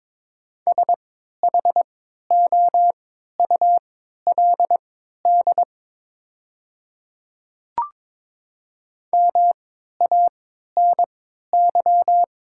Morse code, heard as a keyed tone, about 22 wpm, spaced as for 12 wpm, 700 Hz. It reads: SHOULD MANY
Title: 3x Character Spacing - Top 300 Words - 22wpm